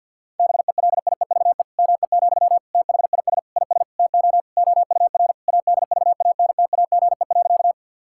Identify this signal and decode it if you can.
Morse code, heard as a keyed tone, about 50 wpm, 700 Hz. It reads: BELIEVE ME, THIS IS TO OUR ADVANTAGE.
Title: Sentences from Top 1000 Words - Review Part 6 - 50wpm